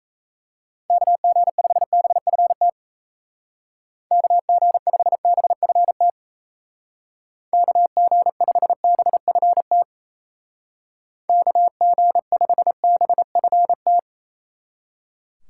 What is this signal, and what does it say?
Morse code, heard as a keyed tone, about 34 wpm, 700 Hz. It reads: KG5BFT KG5BFT KG5BFT KG5BFT
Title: Speed Racing - USA 2x3 Call Signs 28wpm